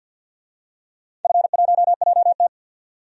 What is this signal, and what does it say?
Morse code, heard as a keyed tone, about 48 wpm, 700 Hz. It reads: F1JT
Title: DXCC-100 Call Signs - 4 Characters - 50wpm